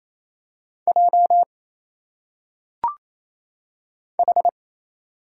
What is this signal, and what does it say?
Morse code, heard as a keyed tone, about 27 wpm, 700 Hz. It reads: J H
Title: Single Letters - Phonetic Pronunciation 28wpm